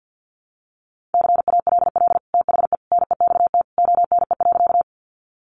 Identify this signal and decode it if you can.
Morse code, heard as a keyed tone, about 48 wpm, 700 Hz. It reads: CALL THE NEXT ONE.
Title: Sentences from Top 300 Words 50wpm